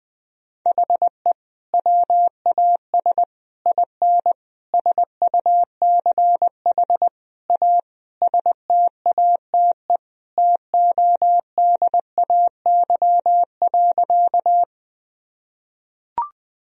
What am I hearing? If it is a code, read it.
Morse code, heard as a keyed tone, about 20 wpm, 700 Hz. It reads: HE WAS IN SUCH A STATE TODAY.
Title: Sentences from Top 1000 Words - Review Part 2 - 20wpm